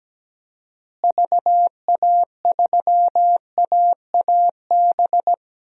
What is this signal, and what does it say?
Morse code, heard as a keyed tone, about 17 wpm, 700 Hz. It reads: VA3AAB
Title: DXCC-100 Call Signs - 6 Characters - Repeated 3x - 17wpm